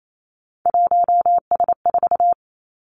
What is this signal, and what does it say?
Morse code, heard as a keyed tone, about 28 wpm, 700 Hz. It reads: 1S4
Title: Three Challenging Letter-Number Combinations 28wpm